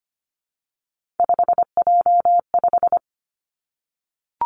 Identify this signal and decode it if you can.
Morse code, heard as a keyed tone, about 25 wpm, 700 Hz. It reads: 5J5
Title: Three Challenging Letter-Number Combinations 25wpm